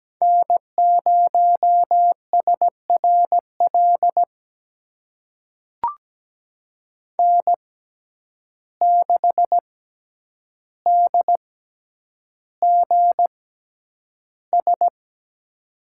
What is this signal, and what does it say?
Morse code, heard as a keyed tone, about 17 wpm, spaced as for 9 wpm, 700 Hz. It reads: N0SRL N6DGS
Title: USA Call Signs - 5 Characters - 6x Character Spacing - 17wpm